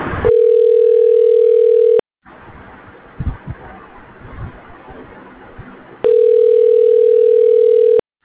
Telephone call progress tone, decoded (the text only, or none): ringback tone